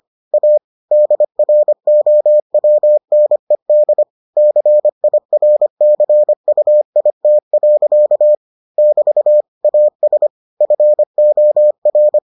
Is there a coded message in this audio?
Yes: Morse code from 0.3 to 12.2 s, heard as a keyed tone, about 25 wpm, 600 Hz. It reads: A DROWNED CIRCUIT. <BT> AS FOR